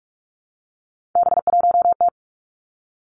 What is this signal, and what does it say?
Morse code, heard as a keyed tone, about 46 wpm, 700 Hz. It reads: B1T
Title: Three Challenging Letter-Number Combinations 45wpm